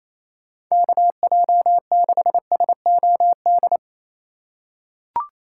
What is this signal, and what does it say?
Morse code, heard as a keyed tone, about 28 wpm, 700 Hz. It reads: KJ6SOD
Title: USA Call Signs - 6 Characters - 3x Character Spacing - 28wpm